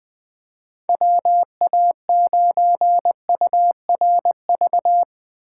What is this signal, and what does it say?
Morse code, heard as a keyed tone, about 22 wpm, 700 Hz. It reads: WA9URV
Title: Speed Racing - USA 2x3 Call Signs 20wpm